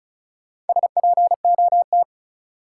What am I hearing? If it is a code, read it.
Morse code, heard as a keyed tone, about 36 wpm, 700 Hz. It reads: SPOT
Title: Sentences from Top 1100 Words - New Word Drill Alt - 35wpm